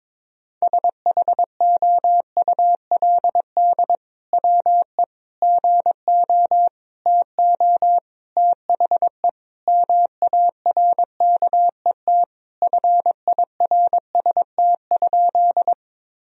Morse code, heard as a keyed tone, about 22 wpm, 700 Hz. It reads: SHOULD WE GO TO THE MARKET FIRST?
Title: Sentences from Top 1000 Words - Review Part 3 - 22wpm